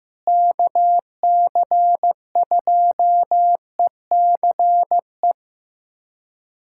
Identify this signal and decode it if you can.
Morse code, heard as a keyed tone, about 15 wpm, 700 Hz. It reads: KC2ECE
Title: USA Callsign DE Callsign 15wpm